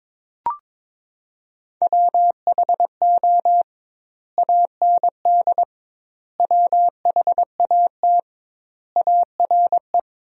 Morse code, heard as a keyed tone, about 22 wpm, 700 Hz. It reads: WHO AND WHAT ARE
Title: 2x Word-Spacing - Sentences from Top 100 Words - 22wpm